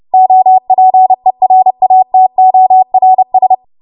Morse code, heard as a keyed tone, about 30 wpm, 750 Hz. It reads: OPERATORS